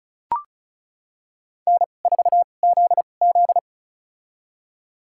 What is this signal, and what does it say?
Morse code, heard as a keyed tone, about 34 wpm, spaced as for 25 wpm, 700 Hz. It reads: N4ZZ
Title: USA Call Signs - 4 Characters - 2x Character Spacing - 35wpm